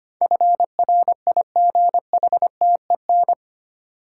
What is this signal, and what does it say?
Morse code, heard as a keyed tone, about 25 wpm, 700 Hz. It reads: FRIGHTEN
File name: 8 Letter Words - First 3 Letters Given - 25wpm